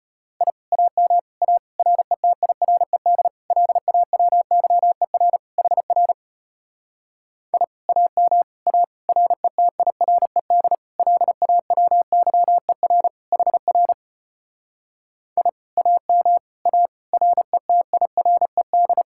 Morse code, heard as a keyed tone, about 33 wpm, 700 Hz. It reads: I AM A RETIRED LAWYER HR I AM A RETIRED LAWYER HR I AM A RETIRED